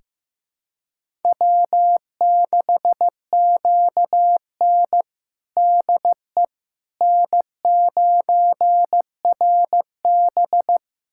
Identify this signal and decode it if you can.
Morse code, heard as a keyed tone, about 16 wpm, 700 Hz. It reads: W6QN DE N9RB
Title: USA Callsign DE Callsign - 4 Characters - 15wpm